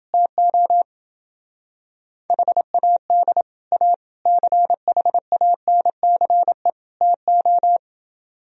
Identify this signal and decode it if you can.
Morse code, heard as a keyed tone, about 28 wpm, 700 Hz. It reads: TO HAD A CHANCE TO